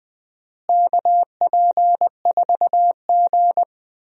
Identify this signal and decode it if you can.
Morse code, heard as a keyed tone, about 20 wpm, 700 Hz. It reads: KP4G
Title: DXCC-100 Call Signs - 4 Characters - Repeated 3x - 20wpm